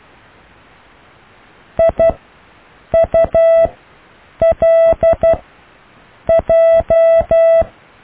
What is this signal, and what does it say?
Morse code, heard as a keyed tone, about 12 wpm, 660 Hz. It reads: I U L J
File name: cw / IJLMU